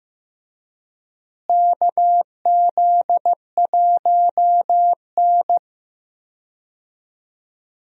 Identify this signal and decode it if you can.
Morse code, heard as a keyed tone, about 15 wpm, 700 Hz. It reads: KZ1N
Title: USA Call Signs - 4 Characters - Repeated 2x - 15wpm